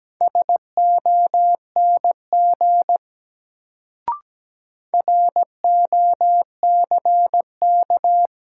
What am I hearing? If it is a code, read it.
Morse code, heard as a keyed tone, about 17 wpm, 700 Hz. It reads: SONG ROCK
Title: Sentences from Top 400 Words - New Word Drill - 17wpm